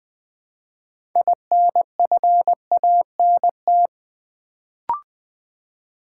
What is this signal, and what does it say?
Morse code, heard as a keyed tone, about 20 wpm, 700 Hz. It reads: INFANT
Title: Build A Word - 3 Letters At A Time - 6-Letter Words - 20wpm